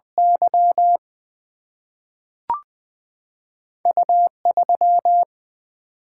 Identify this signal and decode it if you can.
Morse code, heard as a keyed tone, about 20 wpm, 700 Hz. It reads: Y U 3
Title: Single-Letter Single-Number 20wpm